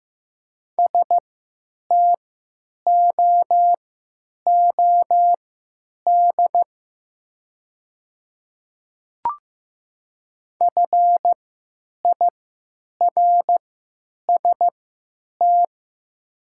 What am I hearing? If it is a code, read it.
Morse code, heard as a keyed tone, about 15 wpm, spaced as for 9 wpm, 700 Hz. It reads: STOOD FIRST
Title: 3x Character Spacing - Top 500 Words - 15wpm